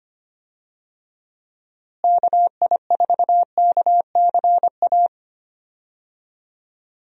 Morse code, heard as a keyed tone, about 25 wpm, 700 Hz. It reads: KI4KCA
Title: USA Call Signs - 6 Characters - Repeated 3x - 25wpm